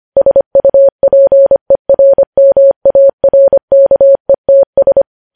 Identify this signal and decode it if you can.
Morse code, heard as a keyed tone, about 25 wpm, 550 Hz. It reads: SUPERMARKETS
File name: podcast / News headlines-1x-25x25